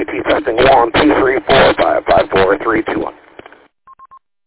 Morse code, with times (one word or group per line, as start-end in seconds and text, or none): none